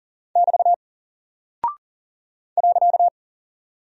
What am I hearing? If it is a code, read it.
Morse code, heard as a keyed tone, about 40 wpm, 700 Hz. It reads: <BT> .
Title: Single Character (Letter:Number:Punctuation:Prosign) 40wpm